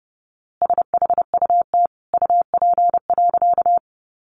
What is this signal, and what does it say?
Morse code, heard as a keyed tone, about 30 wpm, 700 Hz. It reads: SHUT UP.
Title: A Sound Of Thunder 30wpm